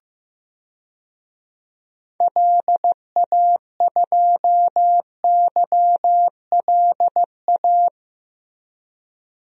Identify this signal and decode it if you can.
Morse code, heard as a keyed tone, about 14 wpm, 700 Hz. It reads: LA2YLA
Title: DXCC-100 Call Signs - 6 Characters - Repeated 3x - 15wpm